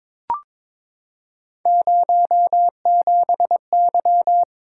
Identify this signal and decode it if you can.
Morse code, heard as a keyed tone, about 22 wpm, 700 Hz. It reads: 07Y
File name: practice / 22wpm / Three Letter-Number Combinations - 22wpm